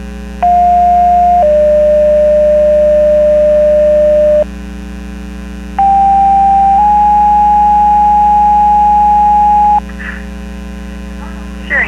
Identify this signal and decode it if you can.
Two-tone page, heard as a one-second tone, then a three-second tone, about 710 then 600 Hz, about 790 then 830 Hz. It is heard from 0.4 to 9.8 s.
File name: Portage 154.385 Paging